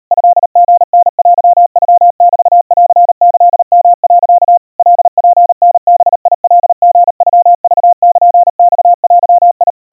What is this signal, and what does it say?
Morse code (Japanese Wabun code), heard as a keyed tone, about 38 wpm, 700 Hz. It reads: トリタテノメンキヨ、 カツタバカリノクルマデ
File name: kawa / Morse sample / kamyu 140 cpm 700 Hz Crispy